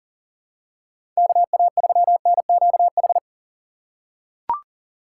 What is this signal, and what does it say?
Morse code, heard as a keyed tone, about 40 wpm, 700 Hz. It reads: KA3NQH
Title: USA Call Signs - 6 Characters - 2x Character Spacing - 40wpm